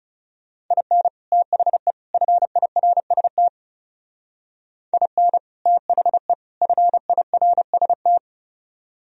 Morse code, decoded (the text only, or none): IN THE FIRST IN THE FIRST